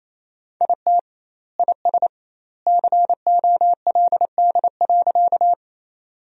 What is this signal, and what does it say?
Morse code, heard as a keyed tone, about 28 wpm, 700 Hz. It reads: IT IS COLD.
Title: 2x Word-Spacing - Sentences from Top 400 Words - 28wpm